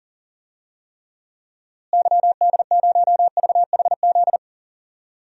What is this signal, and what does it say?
Morse code, heard as a keyed tone, about 40 wpm, 700 Hz. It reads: YD0VHZ
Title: DXCC-100 Call Signs - 6 Characters - Repeated 2x - 40wpm